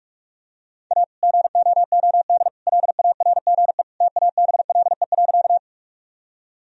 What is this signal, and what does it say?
Morse code, heard as a keyed tone, about 42 wpm, 700 Hz. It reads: A GOOD LARGE TABLE.
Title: Sentences from Top 1000 Words - Review Part 2 - 45wpm